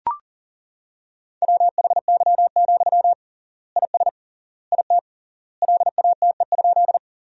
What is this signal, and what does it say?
Morse code, heard as a keyed tone, about 40 wpm, 700 Hz. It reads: WHY, IS IT LATE?
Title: 3x Word-Spacing - Sentences from Top 300 Words - 40wpm